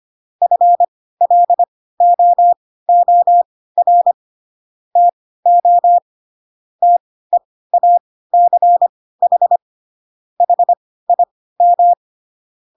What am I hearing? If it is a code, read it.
Morse code, heard as a keyed tone, about 25 wpm, spaced as for 16 wpm, 700 Hz. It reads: FLOOR TO TEACH HIM
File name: Pinocchio / Pinocchio ch3 10 25 ik1hge